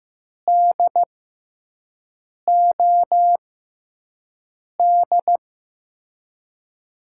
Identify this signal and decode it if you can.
Morse code, heard as a keyed tone, about 15 wpm, spaced as for 5 wpm, 700 Hz. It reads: DOD
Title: USA Call Signs - 6 Characters - 6x Character Spacing - 15wpm